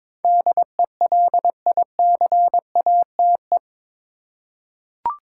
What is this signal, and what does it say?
Morse code, heard as a keyed tone, about 22 wpm, 700 Hz. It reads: DELICATE
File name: practice / 22wpm / Eight Letter Words - 22wpm